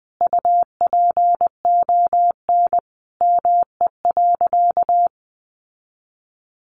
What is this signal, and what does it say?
Morse code, heard as a keyed tone, about 20 wpm, 700 Hz. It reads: UPON ME.
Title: Sentences from Top 1300 Words 20wpm